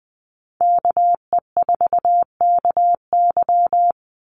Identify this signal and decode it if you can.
Morse code, heard as a keyed tone, about 20 wpm, 700 Hz. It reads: KE4KY
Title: Call Signs Repeated 2x - 20wpm